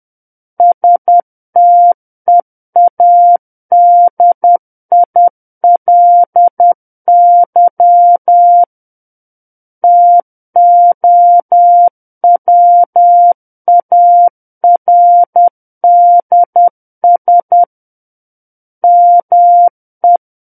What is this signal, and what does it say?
Morse code, heard as a keyed tone, about 10 wpm, 700 Hz. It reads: STEADILY TOWARDS ME
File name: Book One / 01-Chapter 1 - 10 WPM